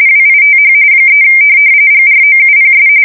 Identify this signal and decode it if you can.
RTTY, about 45.45 baud. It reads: WITH 3 GROUPS
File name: fsk170 / USS-New-Jersey